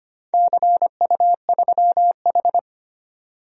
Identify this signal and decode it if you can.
Morse code, heard as a keyed tone, about 25 wpm, 700 Hz. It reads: CU3H
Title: DXCC-100 Call Signs - 4 Characters - 25wpm